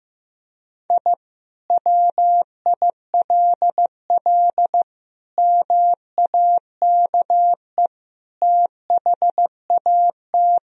Morse code, heard as a keyed tone, about 16 wpm, 700 Hz. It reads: I WILL MAKE THAT
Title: Sentences from Top 1000 Words - Review Part 3 - 15wpm